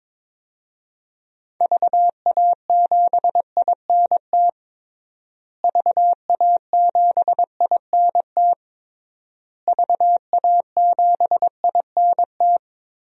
Morse code, heard as a keyed tone, about 22 wpm, 700 Hz. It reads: VA7INT VA7INT VA7INT
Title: DXCC-100 Call Signs - 6 Characters - Repeated 3x - 22wpm